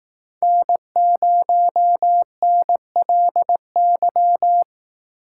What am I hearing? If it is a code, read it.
Morse code, heard as a keyed tone, about 18 wpm, 700 Hz. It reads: N0NLY
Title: Speed Racing - USA 1x3 Call Signs 15wpm